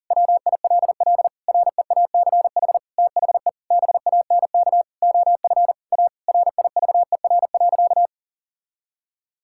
Morse code, heard as a keyed tone, about 37 wpm, 700 Hz. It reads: WILL REACH THE BANK OF A RIVER.
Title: Sentences from Top 1000 Words - Review Part 6 - 40wpm